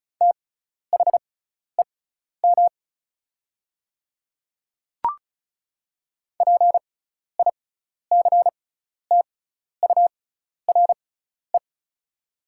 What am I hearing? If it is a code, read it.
Morse code, heard as a keyed tone, about 34 wpm, spaced as for 12 wpm, 690 Hz. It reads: THEM PICTURE
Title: 6x Character Spacing - Top 200 Words - 35wpm